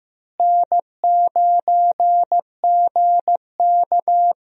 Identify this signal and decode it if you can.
Morse code, heard as a keyed tone, about 15 wpm, 700 Hz. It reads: N9GK
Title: USA Call Signs - 4 Characters - Repeated 3x - 15wpm